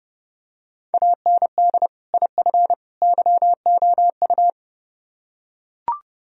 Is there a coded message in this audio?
Yes: Morse code from 0.9 to 4.5 s, heard as a keyed tone, about 30 wpm, 700 Hz. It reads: AND IF YOU